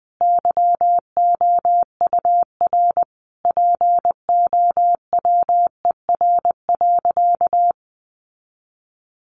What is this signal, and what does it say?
Morse code, heard as a keyed tone, about 20 wpm, 700 Hz. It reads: YOUR POWER.
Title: Sentences from Top 600 Words 20wpm